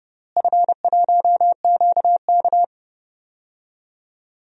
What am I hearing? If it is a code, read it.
Morse code, heard as a keyed tone, about 30 wpm, 700 Hz. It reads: F1QK
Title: DXCC-100 Call Signs - 4 Characters - Repeated 2x - 30wpm